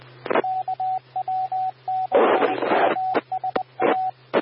Morse code, heard as a keyed tone, about 20 wpm, 750 Hz, with radio noise. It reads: KW6USA
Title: W6NUT 147.450 MHz Repeater